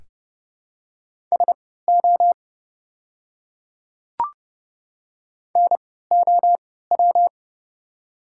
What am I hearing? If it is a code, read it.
Morse code, heard as a keyed tone, about 32 wpm, spaced as for 17 wpm, 700 Hz. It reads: SO NOW
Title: 3x Character Spacing - Top 100 Words - 30wpm